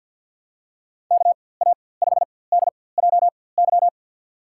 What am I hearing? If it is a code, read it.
Morse code, heard as a keyed tone, about 53 wpm, spaced as for 24 wpm, 700 Hz. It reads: KA5DJY
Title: USA Call Signs - 6 Characters - 4x Character Spacing - 50wpm